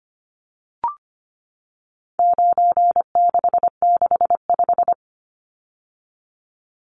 Morse code, heard as a keyed tone, about 25 wpm, 700 Hz. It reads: 9665
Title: Numbers - 4-Digit 25wpm